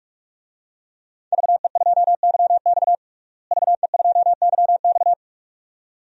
Morse code, heard as a keyed tone, about 46 wpm, 700 Hz. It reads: VE2YX VE2YX
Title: DXCC-100 Call Signs - 5 Characters - Repeated 2x - 45wpm